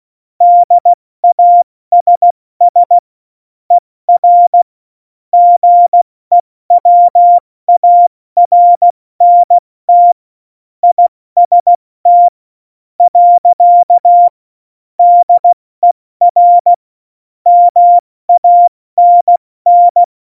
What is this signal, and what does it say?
Morse code, heard as a keyed tone, about 16 wpm, spaced as for 14 wpm, 700 Hz. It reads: DASS ER GEWARNT IST. DER MANN
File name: Morsekurs / Witze / Witz 04. Juni 2019, 70 F80